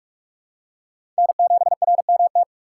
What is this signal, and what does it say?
Morse code, heard as a keyed tone, about 44 wpm, 700 Hz. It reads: N7RMT
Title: USA Call Signs - 3x Character Spacing - 45wpm